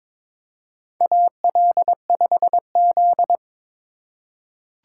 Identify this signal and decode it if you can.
Morse code, heard as a keyed tone, about 22 wpm, 700 Hz. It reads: AL5Z
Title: USA Call Signs - 4 Characters - 2x Character Spacing - 22wpm